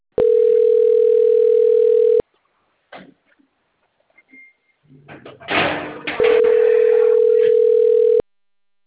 Ringback tone (North America), 440 + 480 Hz.